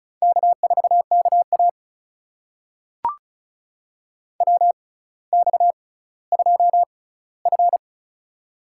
Morse code, heard as a keyed tone, about 36 wpm, spaced as for 18 wpm, 700 Hz. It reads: K4KA WX2F
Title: USA Call Signs - 4 Characters - 6x Character Spacing - 35wpm